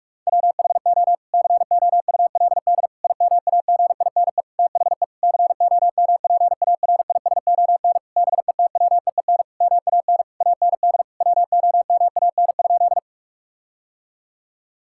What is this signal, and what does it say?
Morse code, heard as a keyed tone, about 46 wpm, 700 Hz. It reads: WHO COULD IMAGINE THE COMPARISON BETWEEN MAN AND WOMAN?
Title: Sentences from Top 1600 Words 45wpm